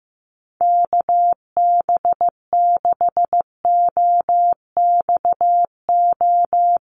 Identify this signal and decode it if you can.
Morse code, heard as a keyed tone, about 15 wpm, 700 Hz. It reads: KB6OXO
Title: USA Call Signs 15wpm